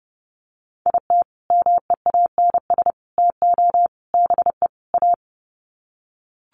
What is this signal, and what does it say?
Morse code, heard as a keyed tone, about 30 wpm, 700 Hz. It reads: IT MEANS TO BE A
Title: Sets of 5 Words Encore2 30wpm